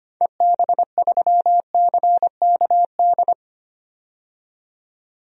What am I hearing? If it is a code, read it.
Morse code, heard as a keyed tone, about 25 wpm, 690 Hz. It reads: EB3CKD
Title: DXCC-100 Call Signs - 6 Characters - 25wpm